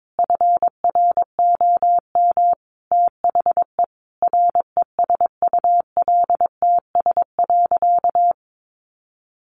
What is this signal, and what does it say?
Morse code, heard as a keyed tone, about 22 wpm, 700 Hz. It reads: FROM THE RESULTS.
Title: A Sound Of Thunder 22wpm